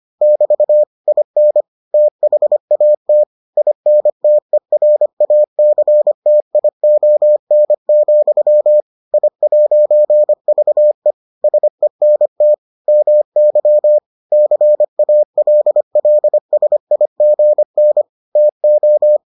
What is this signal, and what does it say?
Morse code, heard as a keyed tone, about 25 wpm, 600 Hz. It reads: <BT> IN THAT INTERACTION, I'VE SENT MY CALLSIGN TO